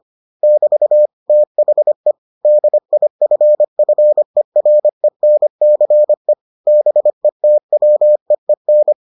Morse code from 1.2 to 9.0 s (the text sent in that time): THE DIFFERENCE BETWEEN